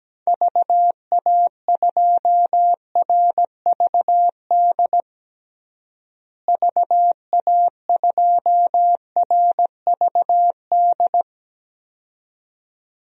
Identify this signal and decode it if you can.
Morse code, heard as a keyed tone, about 17 wpm, 700 Hz. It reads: VA2RVD VA2RVD
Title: DXCC-100 Call Signs - 6 Characters - Repeated 2x - 17wpm